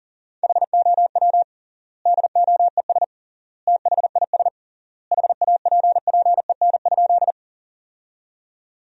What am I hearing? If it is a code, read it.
Morse code, heard as a keyed tone, about 40 wpm, 700 Hz. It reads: HOW DOES THIS HAPPEN?
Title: 3x Word-Spacing - Sentences from Top 400 Words - 40wpm